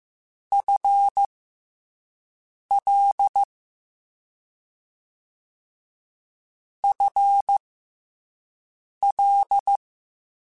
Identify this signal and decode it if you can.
Morse code, heard as a keyed tone, about 15 wpm, spaced as for 5 wpm, 800 Hz. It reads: FL FL
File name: FL / FL mp3s / FL Prefixes 3x2 Twice @5